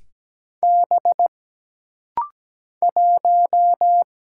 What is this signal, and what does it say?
Morse code, heard as a keyed tone, about 17 wpm, 700 Hz. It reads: B 1
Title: Single Letter-Number - Rapid-Fire 17wpm